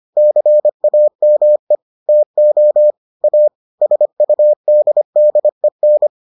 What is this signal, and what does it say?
Morse code, heard as a keyed tone, about 25 wpm, 600 Hz. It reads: CAME TO A SUDDEN